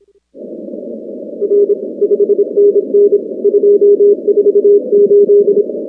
Morse code, heard as a keyed tone, about 27 wpm, 410 Hz. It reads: R 5NN 248
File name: contests / 2019-cw-wpx-kt5j-k5pi- -2330 0-19